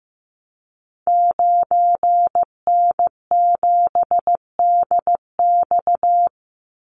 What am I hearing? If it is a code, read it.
Morse code, heard as a keyed tone, about 15 wpm, 700 Hz. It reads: 9N7DX
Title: Call Signs Repeated 2x - 15wpm